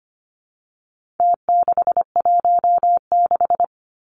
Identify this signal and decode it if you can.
Morse code, heard as a keyed tone, about 25 wpm, 700 Hz. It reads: T616
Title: Numbers - Contest Style - 4-Digit 25wpm